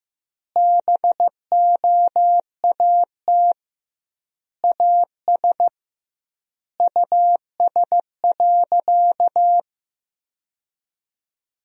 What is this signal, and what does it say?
Morse code, heard as a keyed tone, about 15 wpm, 700 Hz. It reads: BOAT AS US.
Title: 2x Word-Spacing - Sentences from Top 500 Words - 15wpm